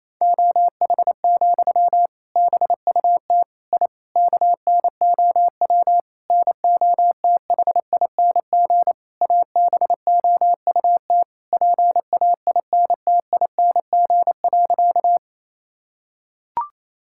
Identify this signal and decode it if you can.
Morse code, heard as a keyed tone, about 28 wpm, 700 Hz. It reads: OH, BUT I KNOW NOTHING ABOUT PAINTING.
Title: Sentences from Top 1000 Words - Review Part 6 - 28wpm